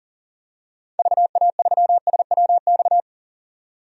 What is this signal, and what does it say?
Morse code, heard as a keyed tone, about 40 wpm, 700 Hz. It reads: VA3SWX
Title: DXCC-100 Call Signs - 6 Characters - 40wpm